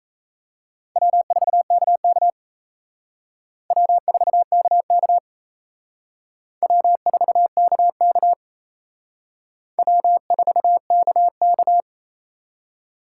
Morse code, heard as a keyed tone, about 34 wpm, 700 Hz. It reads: W4KK W4KK W4KK W4KK
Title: Speed Racing - USA 1x2 Call Signs 28wpm